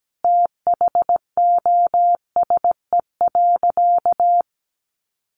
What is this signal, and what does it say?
Morse code, heard as a keyed tone, about 17 wpm, 700 Hz. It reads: THOSE.
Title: A Sound Of Thunder 17wpm